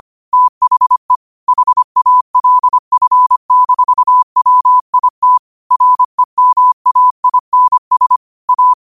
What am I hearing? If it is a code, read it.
Morse code, heard as a keyed tone, about 25 wpm, 1000 Hz. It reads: THE HALF-WIT REMAINS A
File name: qotdfiles / 25wpm